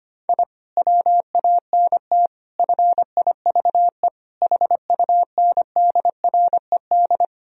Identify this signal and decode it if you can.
Morse code, heard as a keyed tone, about 25 wpm, 700 Hz. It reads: I WANT FIVE HUNDRED
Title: Sentences from Top 1000 Words - Review Part 2 - 25wpm